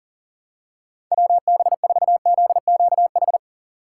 Morse code, heard as a keyed tone, about 37 wpm, 700 Hz. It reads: WB4ZQH
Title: USA Call Signs - 3x Character Spacing - 40wpm